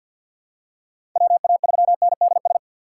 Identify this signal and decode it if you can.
Morse code, heard as a keyed tone, about 46 wpm, 700 Hz. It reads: WA3NDS
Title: USA Call Signs - 6x Character Spacing - 50wpm